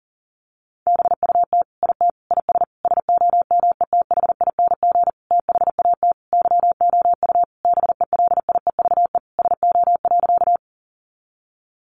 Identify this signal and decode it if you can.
Morse code, heard as a keyed tone, about 40 wpm, 700 Hz. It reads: BUT IT IS SOMETHING THAT YOU BELIEVE SO.
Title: Sentences from Top 1000 Words 40wpm